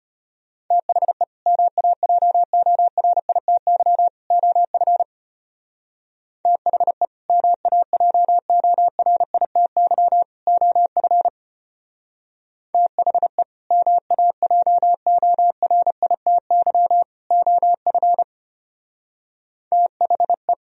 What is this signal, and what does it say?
Morse code, heard as a keyed tone, about 29 wpm, 700 Hz. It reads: THE MAJORITY OF THE MAJORITY OF THE MAJORITY OF THE
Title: Speed-Racing - Sets of 3 Words 25wpm